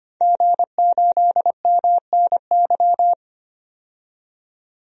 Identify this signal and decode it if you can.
Morse code, heard as a keyed tone, about 25 wpm, 700 Hz. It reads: G8MNY
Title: DXCC-100 Call Signs - 5 Characters - 25wpm